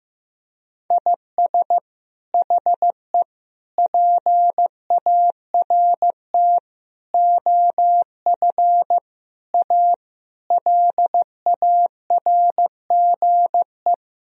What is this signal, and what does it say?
Morse code, heard as a keyed tone, about 15 wpm, 700 Hz. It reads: IS HE PART OF A LARGE
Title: Sentences from Top 1000 Words - Review Part 4 - 15wpm